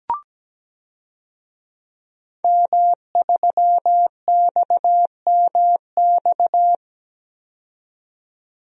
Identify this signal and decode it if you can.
Morse code, heard as a keyed tone, about 17 wpm, 700 Hz. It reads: M3XMX
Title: DXCC-100 Call Signs - 5 Characters - 17wpm